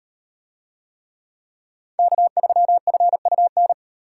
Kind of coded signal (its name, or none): Morse code